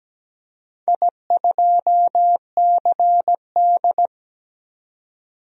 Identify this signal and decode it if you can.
Morse code, heard as a keyed tone, about 17 wpm, 700 Hz. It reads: I2CD
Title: DXCC-100 Call Signs - 4 Characters - 17wpm